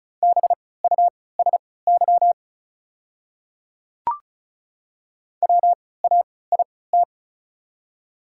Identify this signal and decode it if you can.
Morse code, heard as a keyed tone, about 36 wpm, spaced as for 20 wpm, 690 Hz. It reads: BUSY WAIT
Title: 3x Character Spacing - Top 500 Words - 35wpm